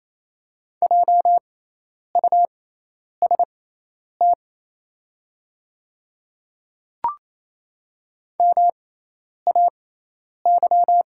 Morse code, heard as a keyed tone, about 28 wpm, spaced as for 10 wpm, 700 Hz. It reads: JUST MAY
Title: 6x Character Spacing - Top 200 Words - 28wpm